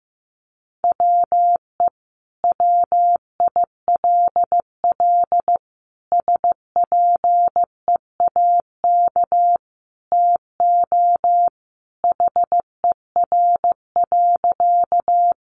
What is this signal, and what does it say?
Morse code, heard as a keyed tone, about 14 wpm, 700 Hz. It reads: WE WILL SPEAK TO HER.
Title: Sentences from Top 1100 Words 15wpm